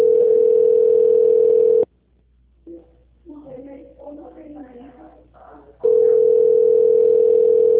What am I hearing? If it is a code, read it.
Ringback tone (North America), 440 + 480 Hz, with speech.
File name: out- -2012- - - .13